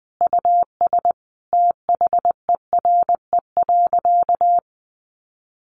Morse code, heard as a keyed tone, about 20 wpm, 700 Hz. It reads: US THERE.
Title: Sentences from Top 1500 Words 20wpm